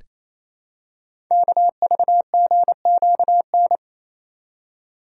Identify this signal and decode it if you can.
Morse code, heard as a keyed tone, about 29 wpm, 700 Hz. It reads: KVGQN
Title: Code Groups - 5 Letters - 28wpm